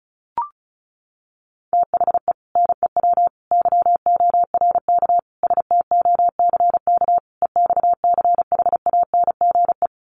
Morse code, heard as a keyed tone, about 34 wpm, 700 Hz. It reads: THE NEW YORK STOCK EXCHANGE
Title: Sets of 5 Words 35wpm